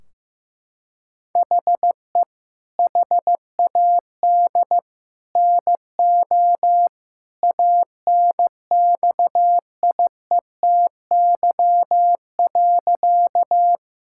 Morse code, heard as a keyed tone, about 15 wpm, 700 Hz. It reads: HE HAD NO ANXIETY.
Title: Sentences from Top 1600 Words 15wpm